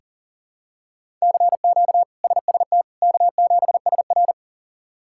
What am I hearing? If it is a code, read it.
Morse code, heard as a keyed tone, about 37 wpm, 700 Hz. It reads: CQ SST K7SR
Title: K1USN SST QSO Practice - 40wpm